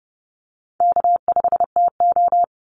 Morse code, heard as a keyed tone, about 30 wpm, 700 Hz. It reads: K5TO
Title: Call Signs - 30wpm